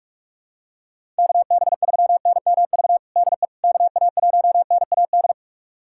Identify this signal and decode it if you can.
Morse code, heard as a keyed tone, about 42 wpm, 700 Hz. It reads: KB3NMV DE KA1NAD
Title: USA Callsign DE Callsign - 6 Characters - 45wpm